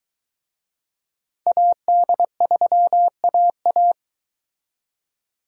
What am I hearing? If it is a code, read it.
Morse code, heard as a keyed tone, about 24 wpm, 700 Hz. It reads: AD3AA